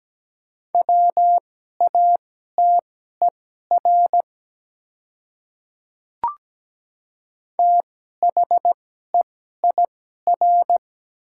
Morse code, heard as a keyed tone, about 17 wpm, spaced as for 12 wpm, 700 Hz. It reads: WATER THEIR
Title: 2x Character Spacing - Top 100 Words - 17wpm